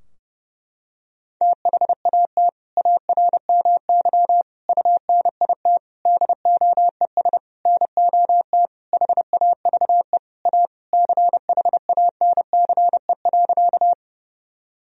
Morse code, heard as a keyed tone, about 30 wpm, 700 Hz. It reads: THAT ARMY UNIT DOES NOT HAVE A CHANCE.
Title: Sentences from Top 1000 Words - Review Part 2 - 30wpm